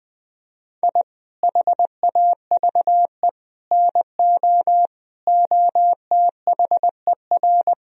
Morse code, heard as a keyed tone, about 20 wpm, 700 Hz. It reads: I HAVE NO OTHER
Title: Sentences from Top 1000 Words - Review Part 4 - 20wpm